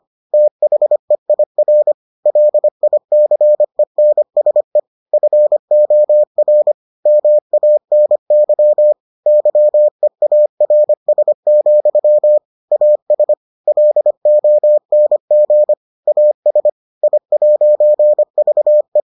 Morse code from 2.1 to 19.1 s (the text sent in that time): LICENSE FOR MANY YEARS, AS LONG AS I'VE